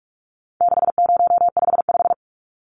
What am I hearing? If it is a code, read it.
Morse code, heard as a keyed tone, about 44 wpm, 700 Hz. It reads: 6055
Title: Numbers - 4-Digit 45wpm